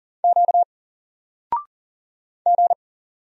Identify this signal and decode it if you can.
Morse code, heard as a keyed tone, about 40 wpm, 700 Hz. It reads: Q G
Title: Single Letter-Number 40wpm